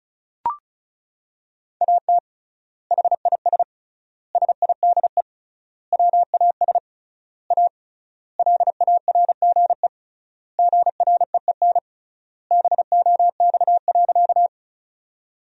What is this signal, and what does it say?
Morse code, heard as a keyed tone, about 34 wpm, 700 Hz. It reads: AT HIS SIDE WAS A LARGE GREEN BOX.
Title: 3x Word-Spacing - Sentences from Top 500 Words - 35wpm